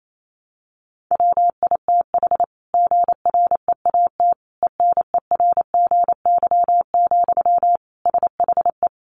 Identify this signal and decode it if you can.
Morse code, heard as a keyed tone, about 28 wpm, 700 Hz. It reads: WITH GREAT ENERGY, SHE
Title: Sentences from Top 900 Words 28wpm